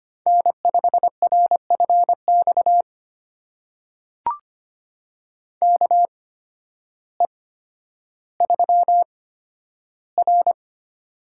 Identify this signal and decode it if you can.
Morse code, heard as a keyed tone, about 25 wpm, spaced as for 12 wpm, 700 Hz. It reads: N5RFX KE3R
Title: USA Call Signs - 8x Character Spacing - 25wpm